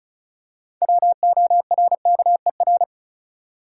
Morse code, heard as a keyed tone, about 34 wpm, 700 Hz. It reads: WORKER
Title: Six Letter Words - 35wpm